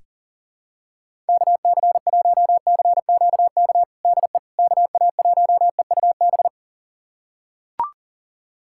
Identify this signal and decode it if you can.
Morse code, heard as a keyed tone, about 40 wpm, 700 Hz. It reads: KC1CQK DE KA1EUB
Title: USA Callsign DE Callsign 40wpm